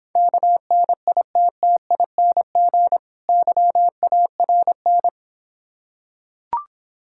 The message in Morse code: KNITTING YARN

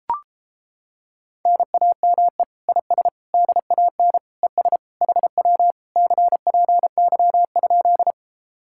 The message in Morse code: NAME IS DAN ES HW CPY?

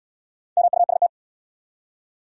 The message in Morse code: MUST

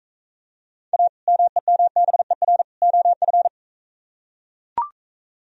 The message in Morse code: A MEMBER OF

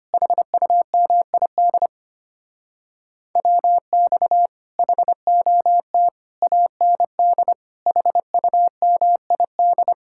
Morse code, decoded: HUMID WX HOT AND HUMID

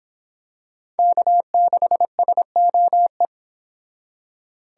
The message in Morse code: K6SOE

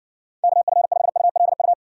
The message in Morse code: KF5ULU

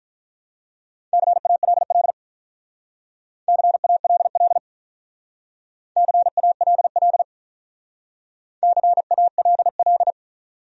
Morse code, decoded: CALL CALL CALL CALL

